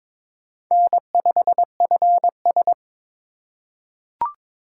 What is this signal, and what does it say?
Morse code, heard as a keyed tone, about 22 wpm, 700 Hz. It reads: N 5 F S
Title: USA Call Signs - 6x Character Spacing - 22wpm